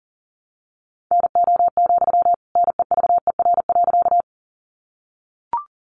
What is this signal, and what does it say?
Morse code, heard as a keyed tone, about 40 wpm, 700 Hz. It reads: NO, NEVER.
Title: Sentences from Top 300 Words 40wpm